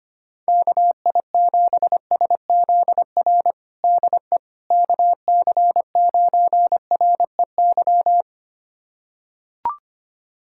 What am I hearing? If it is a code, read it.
Morse code, heard as a keyed tone, about 25 wpm, 700 Hz. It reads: KI7SZR DE KC9REY
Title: USA Callsign DE Callsign - 6 Characters - 25wpm